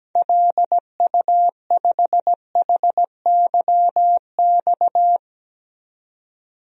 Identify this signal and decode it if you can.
Morse code, heard as a keyed tone, about 17 wpm, 700 Hz. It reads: LU5HYX
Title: DXCC-100 Call Signs - 6 Characters - 17wpm